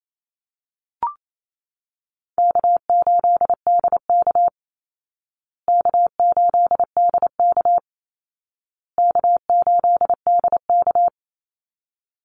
Morse code, heard as a keyed tone, about 28 wpm, 700 Hz. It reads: K8DK K8DK K8DK
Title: Call Signs Repeated 2x - 28wpm